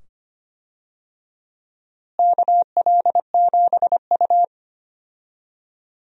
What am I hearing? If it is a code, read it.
Morse code, heard as a keyed tone, about 25 wpm, 700 Hz. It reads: KL7U
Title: DXCC-100 Call Signs - 4 Characters - Repeated 2x - 25wpm